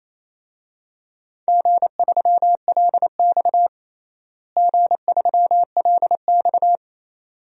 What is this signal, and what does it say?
Morse code, heard as a keyed tone, about 28 wpm, 700 Hz. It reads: G3LX G3LX
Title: DXCC-100 Call Signs - 4 Characters - Repeated 2x - 28wpm